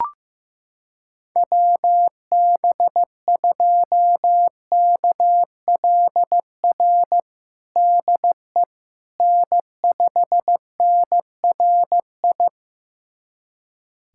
Morse code, heard as a keyed tone, about 15 wpm, 700 Hz. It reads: WB2KLR DE N5NRI
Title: USA Callsign DE Callsign 15wpm